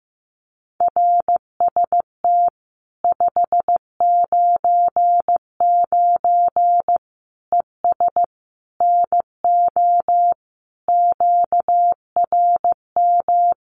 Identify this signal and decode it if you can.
Morse code, heard as a keyed tone, about 15 wpm, 700 Hz. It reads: RST 599 ES NO QRM
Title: Common QSO Phrases 15wpm